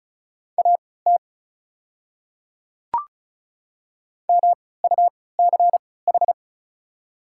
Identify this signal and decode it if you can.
Morse code, heard as a keyed tone, about 36 wpm, spaced as for 20 wpm, 700 Hz. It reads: AT MUCH
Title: 3x Character Spacing - Top 200 Words - 35wpm